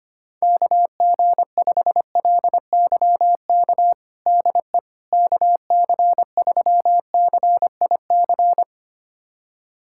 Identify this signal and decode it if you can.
Morse code, heard as a keyed tone, about 25 wpm, 700 Hz. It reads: KG5LYK DE KC3CIC
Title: USA Callsign DE Callsign - 6 Characters - 25wpm